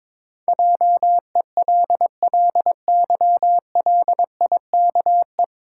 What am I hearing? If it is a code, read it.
Morse code, heard as a keyed tone, about 22 wpm, 700 Hz. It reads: JELLYLIKE
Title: Pangram Challenge v2 - 22wpm